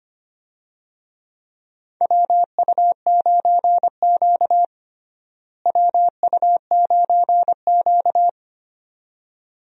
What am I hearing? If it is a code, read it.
Morse code, heard as a keyed tone, about 25 wpm, 700 Hz. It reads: WU9Q WU9Q
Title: USA Call Signs - 4 Characters - Repeated 2x - 25wpm